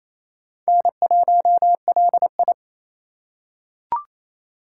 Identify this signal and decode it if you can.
Morse code, heard as a keyed tone, about 28 wpm, 700 Hz. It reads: N1LI
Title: USA Call Signs - 4 Characters - 4x Character Spacing - 28wpm